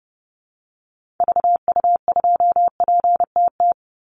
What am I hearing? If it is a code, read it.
Morse code, heard as a keyed tone, about 30 wpm, 700 Hz. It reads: VU2PTT
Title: Call Signs - 30wpm